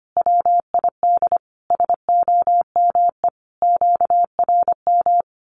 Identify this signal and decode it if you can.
Morse code, heard as a keyed tone, about 25 wpm, 700 Hz. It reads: WID SOME QRM
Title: Common QSO Phrases 25wpm